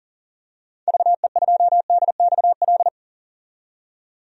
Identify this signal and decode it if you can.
Morse code, heard as a keyed tone, about 40 wpm, 700 Hz. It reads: VE2DXL
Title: DXCC-100 Call Signs - 6 Characters - 40wpm